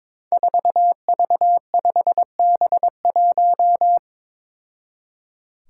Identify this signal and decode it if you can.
Morse code, heard as a keyed tone, about 22 wpm, 700 Hz. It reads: 4V5B1
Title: Five Challenging Letter-Number Combinations 22wpm